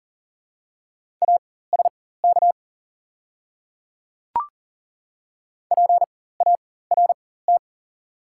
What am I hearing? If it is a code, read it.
Morse code, heard as a keyed tone, about 41 wpm, spaced as for 19 wpm, 700 Hz. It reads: ASK PART
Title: 4x Character Spacing - Top 200 Words - 40wpm